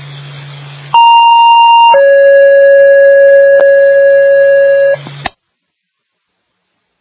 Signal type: two-tone page